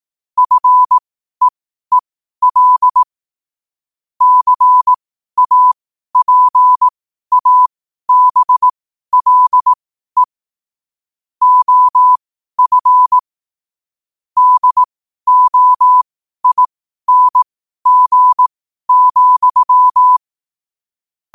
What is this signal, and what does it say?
Morse code, heard as a keyed tone, about 18 wpm, spaced as for 12 wpm, 1000 Hz. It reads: FEEL CAPABLE OF DOING,